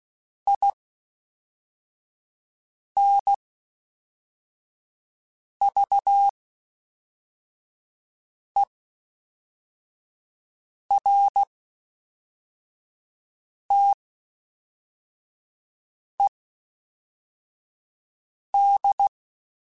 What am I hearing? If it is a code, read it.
Morse code, heard as a keyed tone, about 16 wpm, spaced as for 4 wpm, 800 Hz. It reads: INVERTED